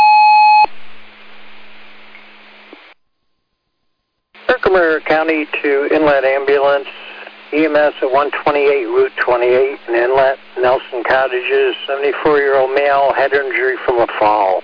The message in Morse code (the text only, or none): none